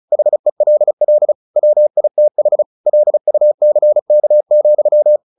Morse code, heard as a keyed tone, about 34 wpm, 600 Hz. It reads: HELL WITH LUCK,
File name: the old man and the sea 2 35wpm